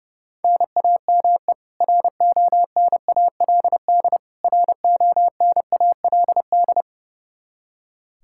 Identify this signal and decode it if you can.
Morse code, heard as a keyed tone, about 32 wpm, 700 Hz. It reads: NAME RONALD RONALD
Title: Practice QSO - Exchange 1 - Beginning - 30wpm